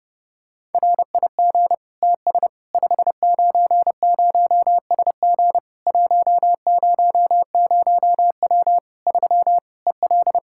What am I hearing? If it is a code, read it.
Morse code, heard as a keyed tone, about 32 wpm, 700 Hz. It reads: RIG TS 590SG 100W 3 EL
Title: K1USN SST QSO Practice 30wpm